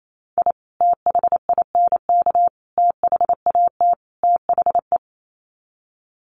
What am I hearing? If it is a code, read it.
Morse code, heard as a keyed tone, about 28 wpm, 700 Hz. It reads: I THINK THAT THE
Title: Sets of 4 Words 28wpm